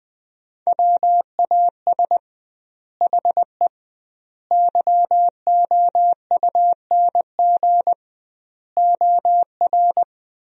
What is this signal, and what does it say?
Morse code, heard as a keyed tone, about 20 wpm, 700 Hz. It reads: WAS HE YOUNG OR
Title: 2x Word-Spacing - Sentences from Top 400 Words - 20wpm